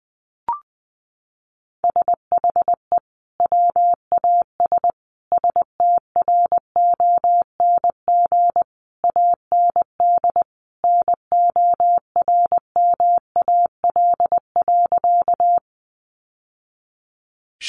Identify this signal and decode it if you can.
Morse code, heard as a keyed tone, about 20 wpm, 700 Hz. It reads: SHE WAS STRONG AND NORMAL.
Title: Sentences from Top 1200 Words 20wpm